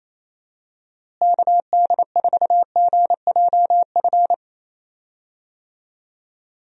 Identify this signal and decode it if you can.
Morse code, heard as a keyed tone, about 28 wpm, 700 Hz. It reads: KD4GJF
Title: USA Call Signs - 6 Characters - Repeated 3x - 28wpm